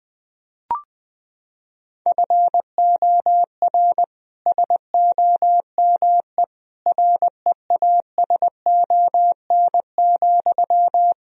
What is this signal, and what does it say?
Morse code, heard as a keyed tone, about 20 wpm, 700 Hz. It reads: FOR SOME REASON,
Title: Sentences from Top 1000 Words - Review Part 3 - 20wpm